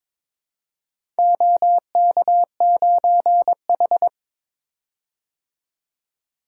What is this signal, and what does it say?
Morse code, heard as a keyed tone, about 22 wpm, 700 Hz. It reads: OK9H
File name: practice / 22wpm / Four Letter-Number Combinations - 22wpm